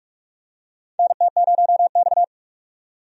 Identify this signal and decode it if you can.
Morse code, heard as a keyed tone, about 46 wpm, 700 Hz. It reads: NT0X
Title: USA Call Signs - 4 Characters - 45wpm